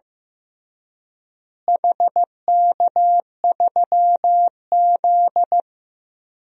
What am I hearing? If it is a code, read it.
Morse code, heard as a keyed tone, about 15 wpm, 700 Hz. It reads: HK3Z
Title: DXCC-100 Call Signs - 4 Characters - Repeated 3x - 15wpm